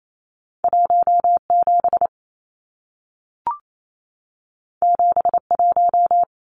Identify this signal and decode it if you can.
Morse code, heard as a keyed tone, about 28 wpm, 700 Hz. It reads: 17 71
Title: Numbers - 2-Digit 28wpm